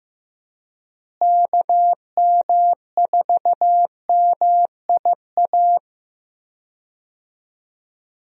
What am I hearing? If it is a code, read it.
Morse code, heard as a keyed tone, about 15 wpm, 700 Hz. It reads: KM4MIA
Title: USA Call Signs - 6 Characters - Repeated 3x - 15wpm